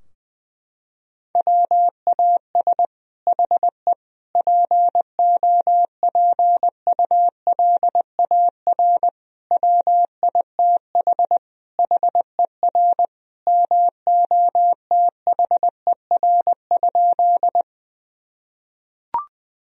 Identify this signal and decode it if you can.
Morse code, heard as a keyed tone, about 21 wpm, 700 Hz. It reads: WAS HE POPULAR WITH HER MOTHER?
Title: Sentences from Top 1000 Words - Review Part 3 - 20wpm